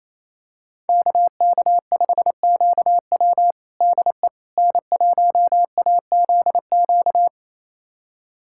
Morse code, heard as a keyed tone, about 28 wpm, 700 Hz. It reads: KK5QW DE N1AZQ
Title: USA Callsign DE Callsign - 5 Characters - 28wpm